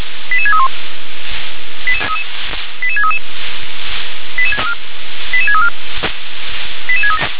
Five-tone selective call (ZVEI), heard as quick runs of five tones, tones of about 70 ms. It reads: 88631 88633 88634